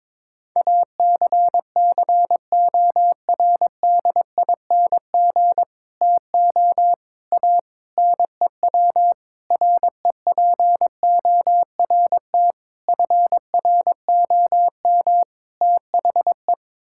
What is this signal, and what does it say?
Morse code, heard as a keyed tone, about 22 wpm, 700 Hz. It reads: ACCORDING TO A NEW REPORT FROM THE
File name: practice / 22wpm / Sets of 7 Words - 22wpm